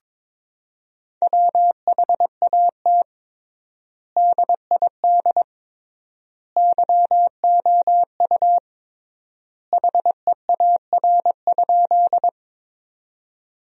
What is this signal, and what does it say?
Morse code, heard as a keyed tone, about 21 wpm, 700 Hz. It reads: WHAT DID YOU HEAR?
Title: 3x Word-Spacing - Sentences from Top 300 Words - 22wpm